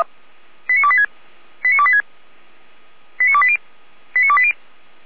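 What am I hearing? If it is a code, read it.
Five-tone selective call (ZVEI), heard as quick runs of five tones, tones of about 70 ms. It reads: 89286 89286 89280 89280
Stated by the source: ZVEI five-tone selective call received by SDR for the Feuerwehr Orlamünde fire service.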